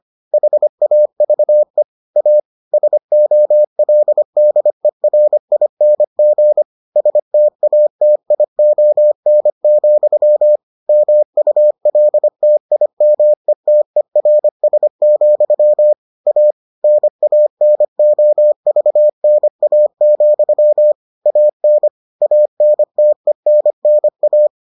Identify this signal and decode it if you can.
Morse code, heard as a keyed tone, about 25 wpm, 600 Hz. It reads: HAVE A SOLDERING STATION, MULTIMETERS, A NANOVNA, AN ANTENNA